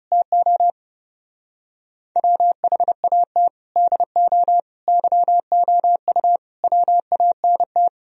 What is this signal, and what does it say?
Morse code, heard as a keyed tone, about 37 wpm, 700 Hz. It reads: TO WHAT DO YOU WANT
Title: Speed-Racing - Sets of 5 Words 30wpm